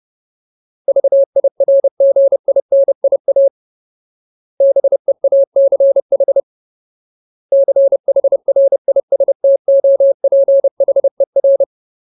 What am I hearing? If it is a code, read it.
Morse code, heard as a keyed tone, about 30 wpm, 550 Hz. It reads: VIRGINIA BEACH CHRISTOPHER
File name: Ultimate Word List / 30WPM Ultimate